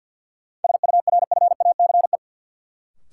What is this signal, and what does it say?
Morse code, heard as a keyed tone, about 48 wpm, 700 Hz. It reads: SURFACE
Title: Sentences from Top 600 Words - New Word Drill - 50wpm